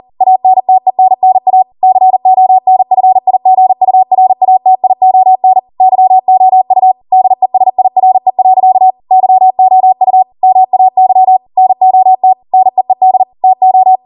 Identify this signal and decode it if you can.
Morse code, heard as a keyed tone, about 40 wpm, 750 Hz. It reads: ANTENNA CONFIGURATION YOU DESIRE. YOU MAY NOT NEED TO